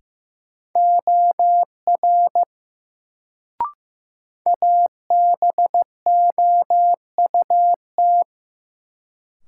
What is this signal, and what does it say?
Morse code, heard as a keyed tone, about 15 wpm, 700 Hz. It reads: OR ABOUT
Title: Sentences from Top 100 Words - New Word Drill - 15wpm